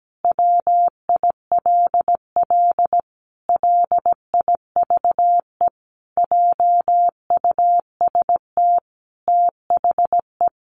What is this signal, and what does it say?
Morse code, heard as a keyed tone, about 17 wpm, 700 Hz. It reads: WILL LIVE JUST THE
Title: Sentences from Top 300 Words 17wpm